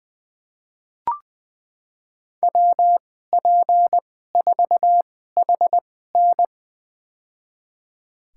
Morse code, heard as a keyed tone, about 20 wpm, spaced as for 14 wpm, 700 Hz. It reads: WP4HN
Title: USA Call Signs - 5 Characters - 2x Character Spacing - 20wpm